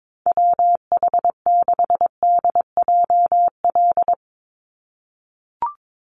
Morse code, heard as a keyed tone, about 22 wpm, 700 Hz. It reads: WH6DJL
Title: USA Call Signs 22wpm